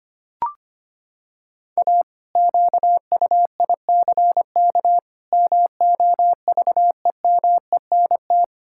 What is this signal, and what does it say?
Morse code, heard as a keyed tone, about 25 wpm, 700 Hz. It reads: A QUICK MOVEMENT
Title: Pangram Challenge v2 - 25wpm